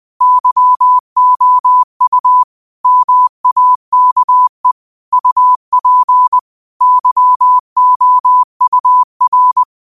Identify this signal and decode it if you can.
Morse code, heard as a keyed tone, about 20 wpm, 1000 Hz. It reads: YOU MAKE UP YOUR
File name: qotdfiles / 20wpm